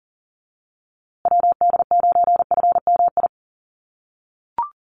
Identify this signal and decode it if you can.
Morse code, heard as a keyed tone, about 40 wpm, 700 Hz. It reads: WD9FMI
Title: USA Call Signs Repeated 3x 40wpm